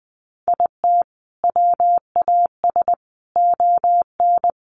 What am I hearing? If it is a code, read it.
Morse code, heard as a keyed tone, about 20 wpm, 700 Hz. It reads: IT WAS ON
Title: Sentences from Top 800 Words 20wpm